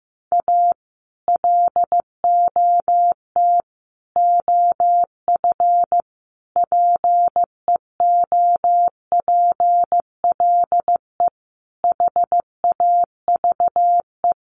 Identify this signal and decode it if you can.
Morse code, heard as a keyed tone, about 15 wpm, 700 Hz. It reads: A LOT OF PEOPLE HAVE
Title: Sets of 5 Words 15wpm